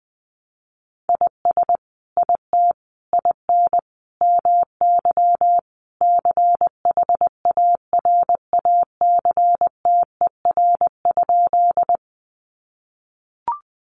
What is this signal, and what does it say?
Morse code, heard as a keyed tone, about 20 wpm, 700 Hz. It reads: IS IT IN MY CHARACTER?
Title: Sentences from Top 1200 Words 20wpm